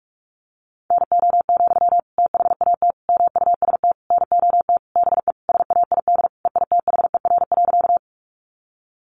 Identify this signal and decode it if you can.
Morse code, heard as a keyed tone, about 44 wpm, 700 Hz. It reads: NO, THAT MUST NOT BE SAID EITHER.
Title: Sentences from Top 600 Words 45wpm